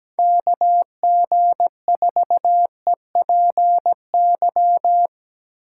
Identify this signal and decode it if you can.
Morse code, heard as a keyed tone, about 19 wpm, 700 Hz. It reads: KG4EPY
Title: Speed Racing - USA 2x3 Call Signs 17wpm